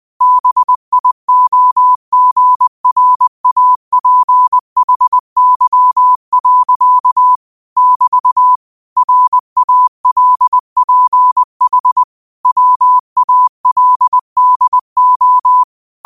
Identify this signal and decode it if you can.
Morse code, heard as a keyed tone, about 20 wpm, 1000 Hz. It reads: BIOGRAPHY. <BT> RALPH WALDO
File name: qotdfiles / 20wpm